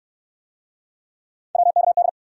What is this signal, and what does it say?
Morse code, heard as a keyed tone, about 66 wpm, 700 Hz. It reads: WKD